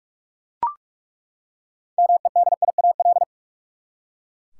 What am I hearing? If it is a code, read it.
Morse code, heard as a keyed tone, about 43 wpm, 700 Hz. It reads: MEDIAL
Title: Practice QSO - Exchange 3 - More Info - 45wpm